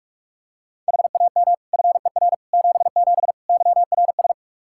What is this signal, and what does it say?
Morse code, heard as a keyed tone, about 48 wpm, 700 Hz. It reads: HAM FER 77 YRS
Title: K1USN SST QSO Practice 45wpm